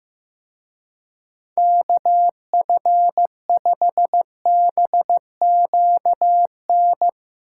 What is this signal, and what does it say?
Morse code, heard as a keyed tone, about 16 wpm, 700 Hz. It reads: KF5BQN
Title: Speed Racing - USA 2x3 Call Signs 15wpm